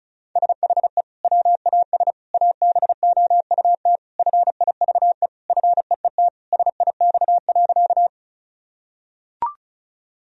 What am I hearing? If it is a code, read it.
Morse code, heard as a keyed tone, about 36 wpm, 700 Hz. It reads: SHE WAS ABOUT FIVE FEET SIX.
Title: Sentences from Top 1000 Words - Review Part 2 - 35wpm